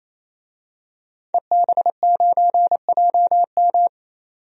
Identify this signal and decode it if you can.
Morse code, heard as a keyed tone, about 28 wpm, 700 Hz. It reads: EB9JM
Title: DXCC-100 Call Signs - 5 Characters - 28wpm